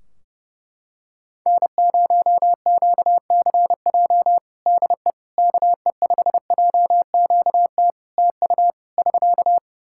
Morse code, heard as a keyed tone, about 32 wpm, 700 Hz. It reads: N0QCJ DE KE5JQT TU <SK>
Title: Practice QSO - Exchange 4 - Ending - 30wpm